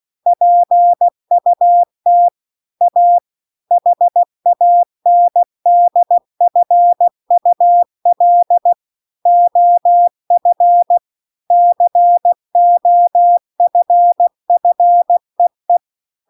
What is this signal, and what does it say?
Morse code, heard as a keyed tone, about 16 wpm, 700 Hz. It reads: PUT A HANDFUL OF COFFEE
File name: CW-The-Big-Twoheaded-River-2-16WPM